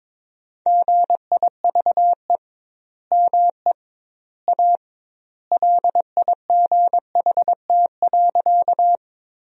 Morse code, heard as a keyed tone, about 22 wpm, 700 Hz. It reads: GIVE ME A LIGHT.
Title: 2x Word-Spacing - Sentences from Top 200 Words - 22wpm